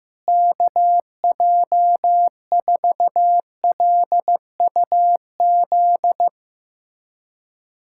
Morse code, heard as a keyed tone, about 16 wpm, 700 Hz. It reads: KJ4LUZ
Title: USA Callsign DE Callsign - 6 Characters - 15wpm